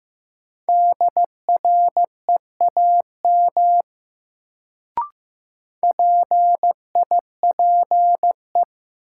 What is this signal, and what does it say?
Morse code, heard as a keyed tone, about 15 wpm, 700 Hz. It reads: DREAM PIPE
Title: Sentences from Top 1500 Words - New Word Drill Alt - 15wpm